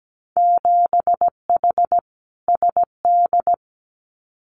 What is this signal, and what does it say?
Morse code, heard as a keyed tone, about 17 wpm, 700 Hz. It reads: 7H SD
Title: Winter Field Day Exchanges Encore 17wpm